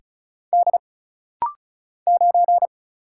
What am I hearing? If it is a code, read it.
Morse code, heard as a keyed tone, about 36 wpm, 700 Hz. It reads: D 9
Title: Single Letter-Number - Rapid-Fire 35wpm